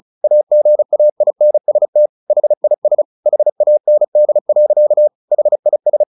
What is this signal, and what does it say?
Morse code, heard as a keyed tone, about 34 wpm, 600 Hz. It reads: AGAINST HIS HAND. HIS
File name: the old man and the sea 2 35wpm